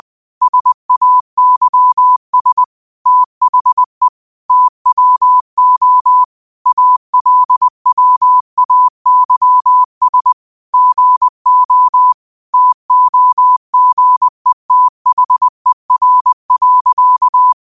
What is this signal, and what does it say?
Morse code, heard as a keyed tone, about 20 wpm, 1000 Hz. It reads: SAYS THE TWO ALWAYS GO TOGETHER.
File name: 20wpm